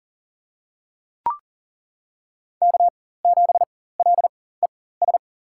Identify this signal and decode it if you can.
Morse code, heard as a keyed tone, about 37 wpm, spaced as for 18 wpm, 700 Hz. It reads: K7LES
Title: USA Call Signs - 5 Characters - 4x Character Spacing - 40wpm